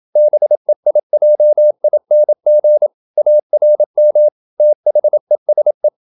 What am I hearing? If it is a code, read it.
Morse code, heard as a keyed tone, about 27 wpm, 600 Hz. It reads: BEIJING ARM THESE
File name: Never 0027wpm